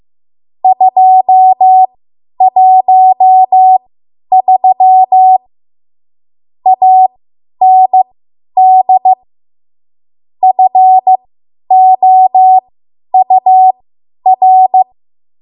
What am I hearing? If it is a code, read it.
Morse code, heard as a keyed tone, about 15 wpm, spaced as for 10 wpm, 750 Hz. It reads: 213 AND FOUR
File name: Morse / 10WPM